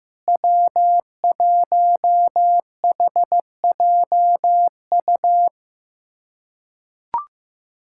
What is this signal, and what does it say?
Morse code, heard as a keyed tone, about 15 wpm, 700 Hz. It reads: W1HJU
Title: USA Callsign DE Callsign - 5 Characters - 15wpm